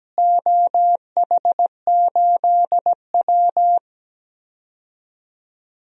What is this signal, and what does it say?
Morse code, heard as a keyed tone, about 17 wpm, 700 Hz. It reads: OH8W
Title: DXCC-100 Call Signs - 4 Characters - Repeated 2x - 17wpm